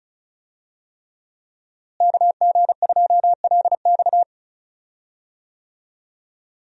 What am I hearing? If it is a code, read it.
Morse code, heard as a keyed tone, about 34 wpm, 700 Hz. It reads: KG2LX
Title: USA Call Signs - 5 Characters - Repeated 3x - 35wpm